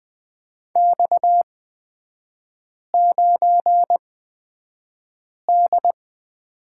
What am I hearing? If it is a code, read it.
Morse code, heard as a keyed tone, about 20 wpm, spaced as for 5 wpm, 690 Hz. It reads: X9D
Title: 229 Lesson 37e - Letter x - Three Characters 20wpm